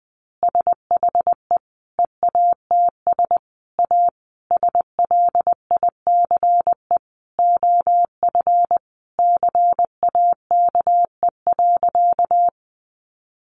Morse code, heard as a keyed tone, about 20 wpm, 700 Hz. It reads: SHE EATS A SLICE OF CAKE.
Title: Easy Phrases 20wpm